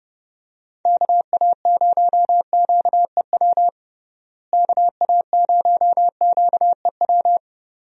Morse code, heard as a keyed tone, about 30 wpm, 700 Hz. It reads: KA0QEW KA0QEW
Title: USA Call Signs - 6 Characters - Repeated 2x - 30wpm